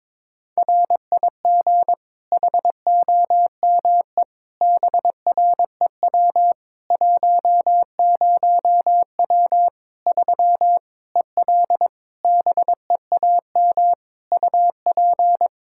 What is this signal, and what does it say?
Morse code, heard as a keyed tone, about 22 wpm, 700 Hz. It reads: RIG HOME BREW 10W 3 EL BEAM UP
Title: Practice QSO - Exchange 2 - Info - 22wpm